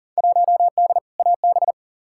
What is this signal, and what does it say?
Morse code, heard as a keyed tone, about 37 wpm, 700 Hz. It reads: 1D AB
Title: Field Day Exchanges Encore - 40wpm